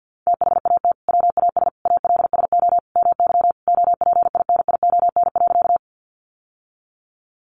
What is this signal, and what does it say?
Morse code, heard as a keyed tone, about 48 wpm, 700 Hz. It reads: THAT WAS ALSO MY OPINION.
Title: Sentences from Top 1200 Words 50wpm